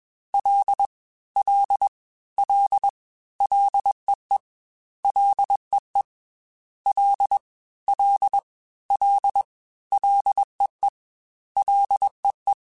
Morse code, heard as a keyed tone, about 21 wpm, 800 Hz. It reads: L L L LEE LEE L L L LEE LEE
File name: F and L / F and L mp3 / F and L Prefixes 3x2 Twice @20